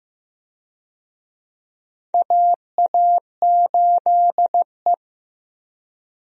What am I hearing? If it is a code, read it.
Morse code, heard as a keyed tone, about 15 wpm, 700 Hz. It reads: AA8E
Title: USA Call Signs - 4 Characters - Repeated 2x - 15wpm